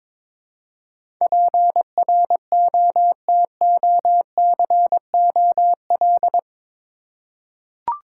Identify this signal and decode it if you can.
Morse code, heard as a keyed tone, about 22 wpm, 700 Hz. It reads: PROTOCOL
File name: practice / 22wpm / Eight Letter Words - 22wpm